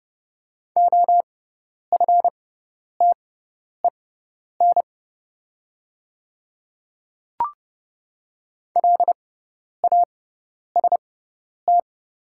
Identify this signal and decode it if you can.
Morse code, heard as a keyed tone, about 30 wpm, spaced as for 10 wpm, 700 Hz. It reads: OFTEN LAST
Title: 6x Character Spacing - Top 300 Words - 30wpm